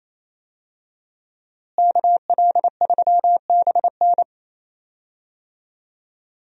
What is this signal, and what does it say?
Morse code, heard as a keyed tone, about 28 wpm, 700 Hz. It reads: KL3BN
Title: USA Call Signs - 5 Characters - Repeated 3x - 28wpm